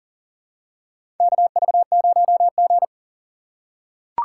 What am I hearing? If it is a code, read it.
Morse code, heard as a keyed tone, about 39 wpm, 690 Hz. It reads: KV0G
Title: USA Call Signs - 4x Character Spacing - 40wpm